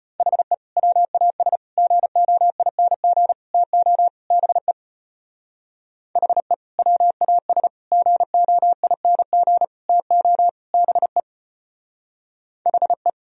Morse code, heard as a keyed tone, about 33 wpm, 700 Hz. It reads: HE WAS GOING TO BE HE WAS GOING TO BE HE